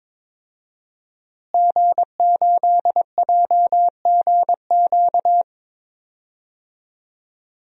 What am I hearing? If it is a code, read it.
Morse code, heard as a keyed tone, about 21 wpm, 700 Hz. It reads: G8JGQ
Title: DXCC-100 Call Signs - 5 Characters - Repeated 2x - 22wpm